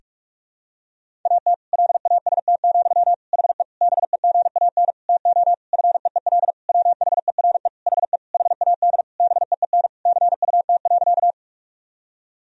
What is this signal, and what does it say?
Morse code, heard as a keyed tone, about 46 wpm, 700 Hz. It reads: AT LAST, HE BEGAN TO FEEL WHERE HE HAD BEEN CUT.
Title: Sentences from Top 1000 Words - Review Part 1 - 45wpm